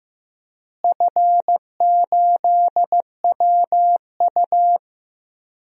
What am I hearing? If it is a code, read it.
Morse code, heard as a keyed tone, about 15 wpm, 700 Hz. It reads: F8WU
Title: DXCC-100 Call Signs - 4 Characters - Repeated 3x - 15wpm